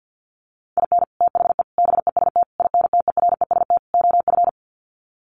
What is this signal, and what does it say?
Morse code, heard as a keyed tone, about 50 wpm, 700 Hz. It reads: IN THE BEST INTEREST OF
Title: Sets of 5 Words Encore 50wpm